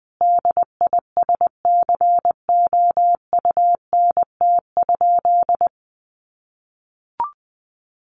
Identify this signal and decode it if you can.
Morse code, heard as a keyed tone, about 20 wpm, 700 Hz. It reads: DISCOUNT?
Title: Sentences from Top 1200 Words 20wpm